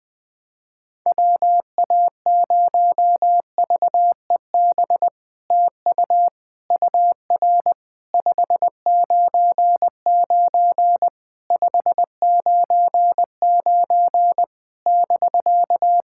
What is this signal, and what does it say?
Morse code, heard as a keyed tone, about 20 wpm, 700 Hz. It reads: WA0VEB TU UR 599 599 <BK>
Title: SOTA QSO Practice 20wpm